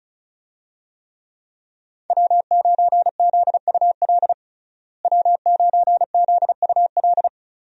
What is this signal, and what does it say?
Morse code, heard as a keyed tone, about 34 wpm, 700 Hz. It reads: W9ZUL W9ZUL
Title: USA Call Signs - 5 Characters - Repeated 3x - 35wpm